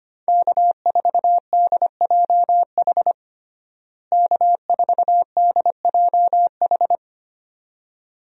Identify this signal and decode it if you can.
Morse code, heard as a keyed tone, about 25 wpm, 700 Hz. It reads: K4DJH K4DJH
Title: USA Call Signs - 5 Characters - Repeated 2x - 25wpm